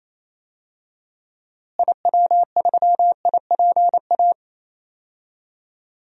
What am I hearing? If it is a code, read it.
Morse code, heard as a keyed tone, about 28 wpm, 700 Hz. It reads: IW3IPA
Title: DXCC-100 Call Signs - 6 Characters - Repeated 3x - 28wpm